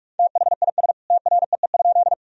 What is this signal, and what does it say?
Morse code, heard as a keyed tone, about 45 wpm, 700 Hz. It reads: THIS TREE?
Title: Sentences from Top 1000 Words - Review Part 1 - 45wpm